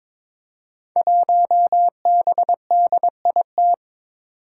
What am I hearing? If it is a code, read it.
Morse code, heard as a keyed tone, about 22 wpm, 700 Hz. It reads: 1BDIT
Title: Five Challenging Letter-Number Combinations 22wpm